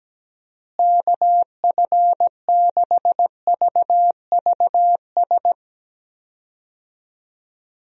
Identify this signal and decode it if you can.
Morse code, heard as a keyed tone, about 17 wpm, 700 Hz. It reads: KF6VVS
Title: USA Call Signs - 6 Characters - Repeated 3x - 17wpm